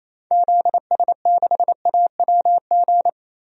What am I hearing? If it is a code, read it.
Morse code, heard as a keyed tone, about 28 wpm, 700 Hz. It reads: ZS6AWG
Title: DXCC-100 Call Signs - 6 Characters - 28wpm